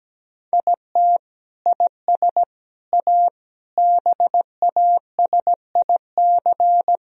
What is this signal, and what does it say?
Morse code, heard as a keyed tone, about 17 wpm, 700 Hz. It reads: IT IS A BASIC
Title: Sentences from Top 1000 Words - Review Part 3 - 17wpm